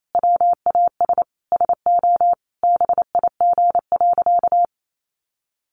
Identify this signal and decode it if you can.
Morse code, heard as a keyed tone, about 28 wpm, 700 Hz. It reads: WAS SO BIG.
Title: Sentences from Top 600 Words 28wpm